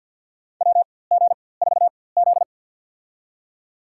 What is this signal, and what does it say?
Morse code, heard as a keyed tone, about 47 wpm, spaced as for 23 wpm, 700 Hz. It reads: WG4Z
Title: USA Call Signs - 4x Character Spacing - 50wpm